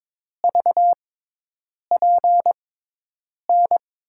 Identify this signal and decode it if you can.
Morse code, heard as a keyed tone, about 23 wpm, spaced as for 8 wpm, 700 Hz. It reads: VPN
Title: USA Call Signs - 6x Character Spacing - 22wpm